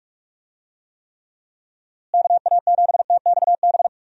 Morse code, heard as a keyed tone, about 42 wpm, 700 Hz. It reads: KA7TXB
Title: USA Call Signs - 6 Characters - Repeated 2x - 45wpm